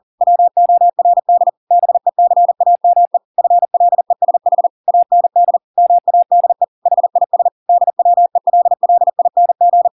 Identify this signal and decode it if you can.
Morse code, heard as a keyed tone, about 40 wpm, 700 Hz. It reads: WORD BECAME FLESH AND MADE HIS DWELLING